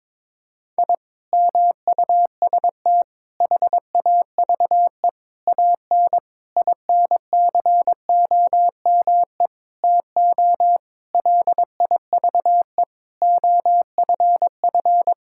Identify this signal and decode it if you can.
Morse code, heard as a keyed tone, about 22 wpm, 700 Hz. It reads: I MUST HAVE AN INCOME TO LIVE OFF
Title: Sentences from Top 1000 Words - Review Part 5 - 22wpm